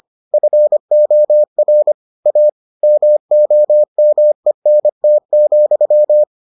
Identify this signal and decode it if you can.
Morse code, heard as a keyed tone, about 25 wpm, 600 Hz. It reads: FOR A MOMENT,